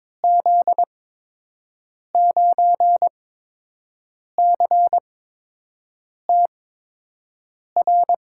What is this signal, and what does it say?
Morse code, heard as a keyed tone, about 22 wpm, spaced as for 6 wpm, 700 Hz. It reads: Z9CTR
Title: USA Call Signs - 6 Characters - 8x Character Spacing - 22wpm